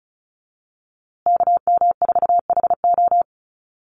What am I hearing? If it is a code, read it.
Morse code, heard as a keyed tone, about 34 wpm, 700 Hz. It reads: KM4HO
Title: USA Call Signs Repeated 3x 35wpm